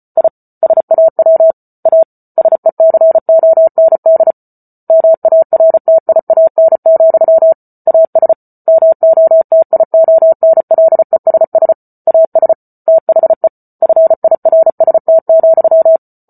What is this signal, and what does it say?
Morse code, heard as a keyed tone, about 34 wpm, 650 Hz. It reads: I SAW A SECOND MARTIAN, AS MOTIONLESS AS THE FIRST,